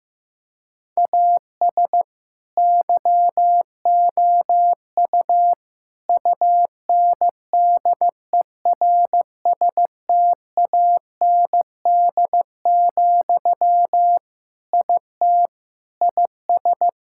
Morse code, heard as a keyed tone, about 15 wpm, 700 Hz. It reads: AS YOU UNDERSTAND, IT IS
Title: Sentences from Top 1000 Words - Review Part 5 - 15wpm